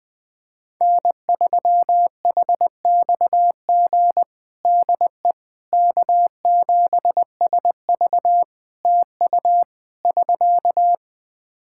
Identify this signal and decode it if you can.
Morse code, heard as a keyed tone, about 20 wpm, 700 Hz. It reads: N3HXG DE K7SV TU <SK>
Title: K1USN SST QSO Practice 20wpm